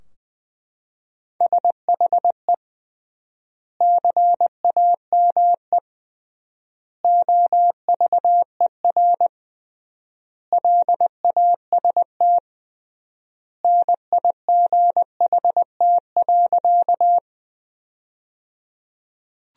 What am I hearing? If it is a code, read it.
Morse code, heard as a keyed tone, about 21 wpm, 700 Hz. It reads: SHE CAME OVER LAST NIGHT.
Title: 3x Word-Spacing - Sentences from Top 300 Words - 20wpm